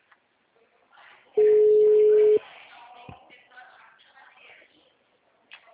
Ringback tone (Europe, 425 Hz).
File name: out- -2030- - - .4812